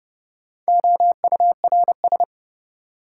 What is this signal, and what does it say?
Morse code, heard as a keyed tone, about 30 wpm, 700 Hz. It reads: OURS
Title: Sentences from Top 1300 Words - New Word Drill Alt - 30wpm